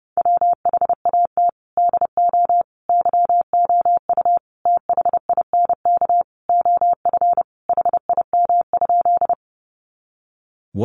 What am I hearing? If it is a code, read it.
Morse code, heard as a keyed tone, about 30 wpm, 700 Hz. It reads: WHAT DO YOU THINK OF HIM?
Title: Sentences from Top 200 Words 30wpm